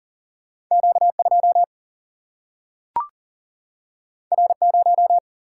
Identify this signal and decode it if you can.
Morse code, heard as a keyed tone, about 40 wpm, 700 Hz. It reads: Q2 R0
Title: Single-Letter Single-Number 40wpm